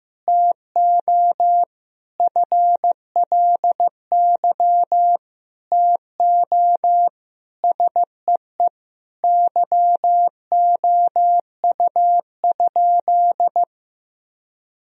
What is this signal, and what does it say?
Morse code, heard as a keyed tone, about 15 wpm, 700 Hz. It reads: TO FLY TO SEE YOU?
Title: Sentences from Top 1000 Words - Review Part 2 - 15wpm